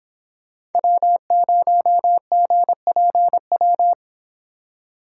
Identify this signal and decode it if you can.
Morse code, heard as a keyed tone, about 28 wpm, 700 Hz. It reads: W0GPW